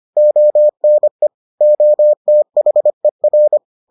Morse code, heard as a keyed tone, about 25 wpm, 600 Hz. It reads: ONE OTHER